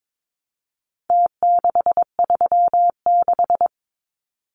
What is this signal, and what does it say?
Morse code, heard as a keyed tone, about 22 wpm, 700 Hz. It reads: T636
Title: Numbers - Contest Style - 4-Digit 22wpm